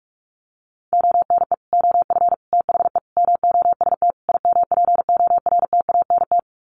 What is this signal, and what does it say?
Morse code, heard as a keyed tone, about 44 wpm, 700 Hz. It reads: ONE OF THE MOST IMPORTANT
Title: Sets of 5 Words 45wpm